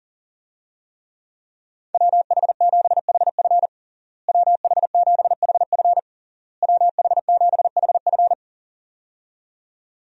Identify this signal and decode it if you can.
Morse code, heard as a keyed tone, about 40 wpm, 700 Hz. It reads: WH7HF WH7HF WH7HF
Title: USA Call Signs - 5 Characters - Repeated 3x - 40wpm